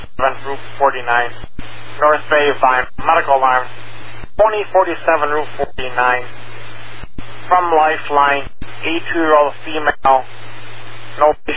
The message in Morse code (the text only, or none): none